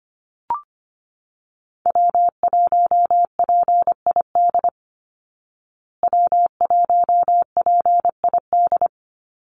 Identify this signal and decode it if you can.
Morse code, heard as a keyed tone, about 25 wpm, 700 Hz. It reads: W1PID W1PID
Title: Call Signs Repeated 3x - 25wpm